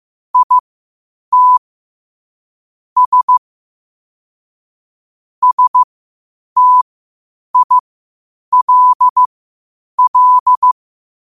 Morse code, heard as a keyed tone, about 15 wpm, spaced as for 8 wpm, 1000 Hz. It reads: IT S STILL
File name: qotdfiles / 8wpm